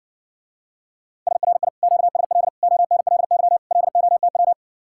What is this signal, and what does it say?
Morse code, heard as a keyed tone, about 60 wpm, 700 Hz. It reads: SRI QSL ONLY LOTW